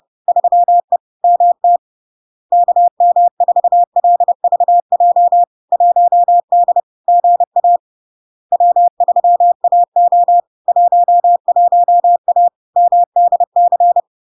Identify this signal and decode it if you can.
Morse code, heard as a keyed tone, about 30 wpm, 700 Hz. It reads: 3E MT KM4LVJ 1D GA W3AO 11A MDC
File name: LARK-FD-2022-Log-in-CW-30wpm-700-Hz